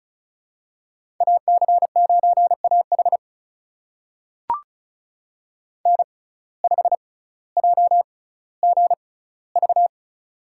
Morse code, heard as a keyed tone, about 36 wpm, spaced as for 18 wpm, 700 Hz. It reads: AC9AH N5JGV
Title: USA Call Signs - 5 Characters - 6x Character Spacing - 35wpm